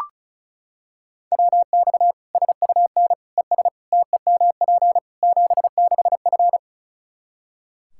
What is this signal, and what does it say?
Morse code, heard as a keyed tone, about 34 wpm, 700 Hz. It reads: WX SUN ES TEMP 76F
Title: Practice QSO - Exchange 2 - Info - 35wpm